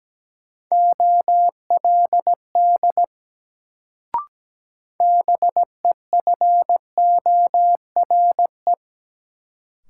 Morse code, heard as a keyed tone, about 17 wpm, 700 Hz. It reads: OLD BEFORE
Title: Sets of 2 Words - New Word Drill - 17wpm